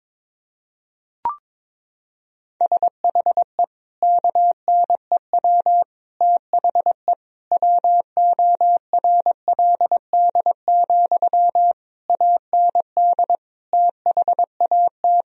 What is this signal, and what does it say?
Morse code, heard as a keyed tone, about 22 wpm, 700 Hz. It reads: SHE KNEW THE WORLD, AND THAT
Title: Sentences from Top 1000 Words - Review Part 4 - 22wpm